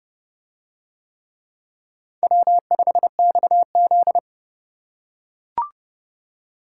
Morse code, heard as a keyed tone, about 29 wpm, 700 Hz. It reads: W5XZ